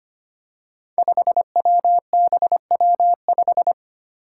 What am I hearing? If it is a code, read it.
Morse code, heard as a keyed tone, about 25 wpm, 700 Hz. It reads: 5WBW5
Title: Five Challenging Letter-Number Combinations 25wpm